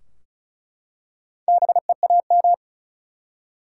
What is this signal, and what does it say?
Morse code, heard as a keyed tone, about 36 wpm, 700 Hz. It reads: BEAM
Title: Top 180 QSO Elements 35wpm